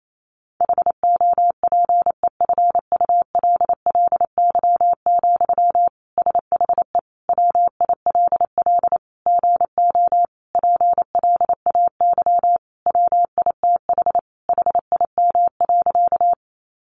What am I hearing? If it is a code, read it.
Morse code, heard as a keyed tone, about 28 wpm, 700 Hz. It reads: HOPEFULLY, SHE WILL GO PLAY WITH HIM.
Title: Sentences from Top 1500 Words 28wpm